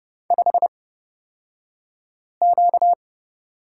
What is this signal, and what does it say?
Morse code, heard as a keyed tone, about 30 wpm, spaced as for 5 wpm, 700 Hz. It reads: 5Q
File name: Lesson 38d - Letter q - Two Characters 30wpm